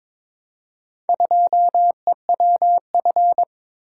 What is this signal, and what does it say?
Morse code, heard as a keyed tone, about 22 wpm, 700 Hz. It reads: 2EWF
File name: practice / 22wpm / Four Letter-Number Combinations - 22wpm